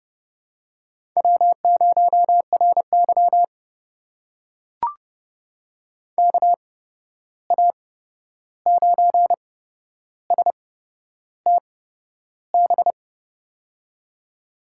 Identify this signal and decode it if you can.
Morse code, heard as a keyed tone, about 31 wpm, spaced as for 11 wpm, 700 Hz. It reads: W0RY KA9STB
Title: USA Call Signs - 8x Character Spacing - 30wpm